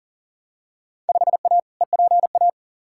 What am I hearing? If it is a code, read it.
Morse code, heard as a keyed tone, about 37 wpm, 700 Hz. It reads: 5A EPA
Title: Field Day Exchanges - 40wpm